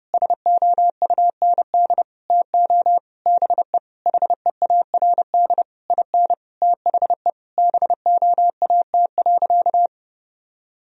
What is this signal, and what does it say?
Morse code, heard as a keyed tone, about 30 wpm, 700 Hz. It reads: SOUND TO BE HEARD IN THE BOAT.
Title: Sentences from Top 1000 Words - Review Part 3 - 30wpm